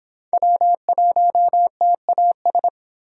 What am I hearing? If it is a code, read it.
Morse code, heard as a keyed tone, about 26 wpm, 700 Hz. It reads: W1TAS